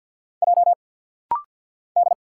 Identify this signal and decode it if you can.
Morse code, heard as a keyed tone, about 50 wpm, 700 Hz. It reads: J D
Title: Single Letters - Rapid-Fire 50wpm